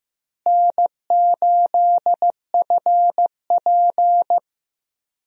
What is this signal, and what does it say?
Morse code, heard as a keyed tone, about 15 wpm, 700 Hz. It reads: N8FP
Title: USA Call Signs - 4 Characters - Repeated 3x - 15wpm